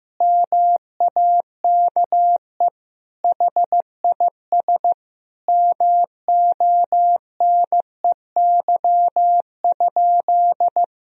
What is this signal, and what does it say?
Morse code, heard as a keyed tone, about 15 wpm, 700 Hz. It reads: MAKE HIS MONEY?
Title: Sentences from Top 400 Words and Abbreviations 15wpm